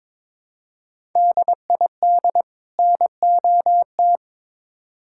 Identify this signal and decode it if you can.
Morse code, heard as a keyed tone, about 22 wpm, 700 Hz. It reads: DID NOT
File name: Build A Phrase - 2-Word Phrases - 22wpm